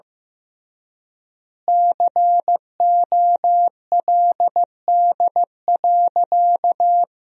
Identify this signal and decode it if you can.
Morse code, heard as a keyed tone, about 15 wpm, 700 Hz. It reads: COLD.
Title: 3x Word-Spacing - Sentences from Top 400 Words - 15wpm